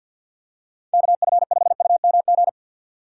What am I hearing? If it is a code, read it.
Morse code, heard as a keyed tone, about 52 wpm, 700 Hz. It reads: KF5UMG
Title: USA Call Signs - 6 Characters - 6x Character Spacing - 50wpm